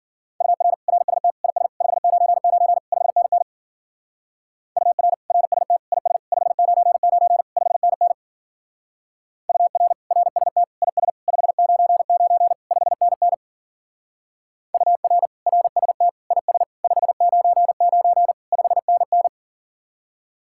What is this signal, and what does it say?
Morse code, heard as a keyed tone, about 49 wpm, 700 Hz. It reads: UR RST IS 599 5NN UR RST IS 599 5NN UR RST IS 599 5NN UR RST IS 599 5NN
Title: Speed-Racing Common QSO Phrases - 40wpm